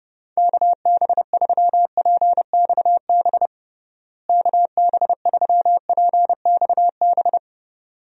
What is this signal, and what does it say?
Morse code, heard as a keyed tone, about 30 wpm, 700 Hz. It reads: KB3PXB KB3PXB
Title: USA Call Signs - 6 Characters - Repeated 2x - 30wpm